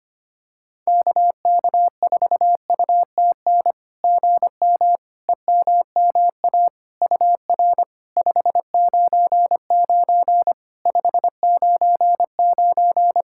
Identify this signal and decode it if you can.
Morse code, heard as a keyed tone, about 25 wpm, 700 Hz. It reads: KK4UTN GM EMMA UR 599 599
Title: SOTA QSO Practice 25wpm